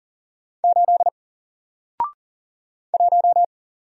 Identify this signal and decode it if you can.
Morse code, heard as a keyed tone, about 40 wpm, 700 Hz. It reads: 8 1
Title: Single Letter-Number Alt 40wpm